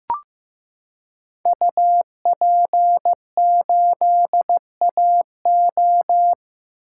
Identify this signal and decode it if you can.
Morse code, heard as a keyed tone, about 15 wpm, 700 Hz. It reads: UP8AO
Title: Code Groups - 5 Letter:Numbers - 15wpm